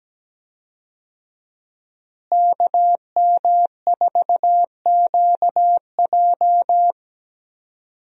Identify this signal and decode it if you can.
Morse code, heard as a keyed tone, about 17 wpm, 700 Hz. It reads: KM4QJ
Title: USA Call Signs - 5 Characters - Repeated 2x - 17wpm